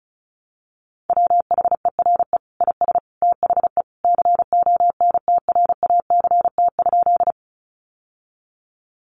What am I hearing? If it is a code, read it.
Morse code, heard as a keyed tone, about 34 wpm, 700 Hz. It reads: WHERE IS THE CONTRACT?
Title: Sentences from Top 1400 Words 35wpm